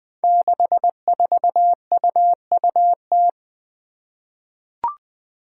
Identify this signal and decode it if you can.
Morse code, heard as a keyed tone, about 20 wpm, 700 Hz. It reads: 64UUT
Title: Five Challenging Letter-Number Combinations 20wpm